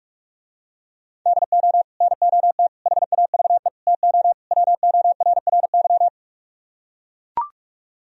Morse code, heard as a keyed tone, about 44 wpm, 700 Hz. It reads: DO NOT HAVE TO WORRY